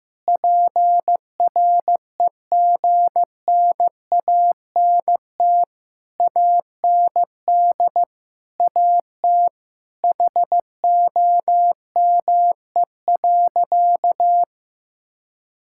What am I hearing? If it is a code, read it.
Morse code, heard as a keyed tone, about 15 wpm, 700 Hz. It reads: PREGNANT AND AT HOME.
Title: Sentences from Top 1000 Words - Review Part 6 - 15wpm